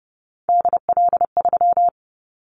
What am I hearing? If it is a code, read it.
Morse code, heard as a keyed tone, about 30 wpm, 700 Hz. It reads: DL3
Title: Callsign Prefixes for Top 25 DXCC - 30wpm